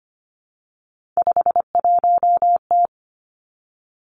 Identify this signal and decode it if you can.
Morse code, heard as a keyed tone, about 25 wpm, 700 Hz. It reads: 51T
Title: Three Challenging Letter-Number Combinations 25wpm